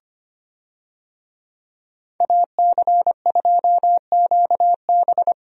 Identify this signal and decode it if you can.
Morse code, heard as a keyed tone, about 25 wpm, 700 Hz. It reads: AC2QB
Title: USA Call Signs - 5 Characters - Repeated 2x - 25wpm